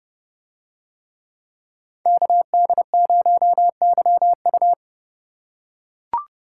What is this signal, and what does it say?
Morse code, heard as a keyed tone, about 29 wpm, 700 Hz. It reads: KD0YU